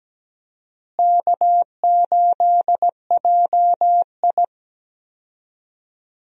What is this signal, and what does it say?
Morse code, heard as a keyed tone, about 17 wpm, 700 Hz. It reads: K8JI
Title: USA Call Signs - 4 Characters - Repeated 3x - 17wpm